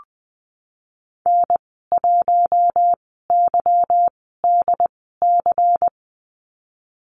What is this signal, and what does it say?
Morse code, heard as a keyed tone, about 20 wpm, spaced as for 15 wpm, 700 Hz. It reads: N1YDC
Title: 179 Lesson 29g - Slash - Call Signs 20wpm